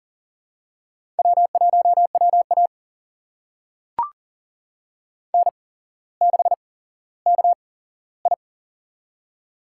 Morse code, heard as a keyed tone, about 38 wpm, spaced as for 17 wpm, 700 Hz. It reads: W1WA N6KI
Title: USA Call Signs - 4 Characters - 8x Character Spacing - 40wpm